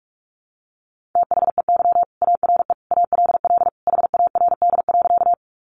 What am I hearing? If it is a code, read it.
Morse code, heard as a keyed tone, about 44 wpm, 700 Hz. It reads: THEY ARE ALL HARD.
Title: Additional Easy Phrases 45wpm